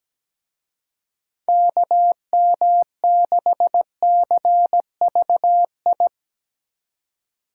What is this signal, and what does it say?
Morse code, heard as a keyed tone, about 19 wpm, 700 Hz. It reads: KM6CVI
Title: Speed Racing - USA 2x3 Call Signs 17wpm